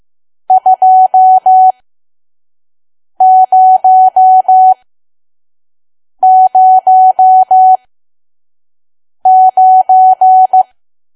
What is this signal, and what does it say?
Morse code, heard as a keyed tone, about 15 wpm, spaced as for 5 wpm, 750 Hz. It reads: 2009
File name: Archive / 5 WPM files / PM